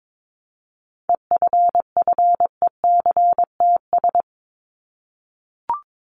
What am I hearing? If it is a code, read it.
Morse code, heard as a keyed tone, about 22 wpm, 700 Hz. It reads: EFFECTS
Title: ES:S Suffix - Top 100 Words with Base Words 22wpm